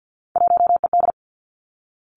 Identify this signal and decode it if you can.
Morse code, heard as a keyed tone, about 50 wpm, 700 Hz. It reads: 1ED
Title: Three Challenging Letter-Number Combinations 50wpm